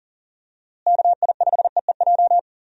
Morse code, heard as a keyed tone, about 38 wpm, 700 Hz. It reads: KI5EEJ
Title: USA Call Signs - 6 Characters - Repeated 3x - 40wpm